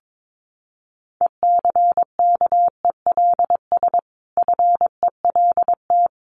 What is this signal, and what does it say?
Morse code, heard as a keyed tone, about 22 wpm, 700 Hz. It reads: ECKELS FELT
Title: A Sound Of Thunder 22wpm